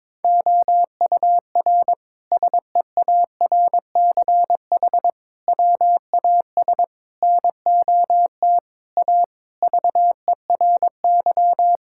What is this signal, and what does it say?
Morse code, heard as a keyed tone, about 22 wpm, 700 Hz. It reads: OUR SEARCH WAS NOT A VERY
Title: Sentences from Top 1000 Words - Review Part 5 - 22wpm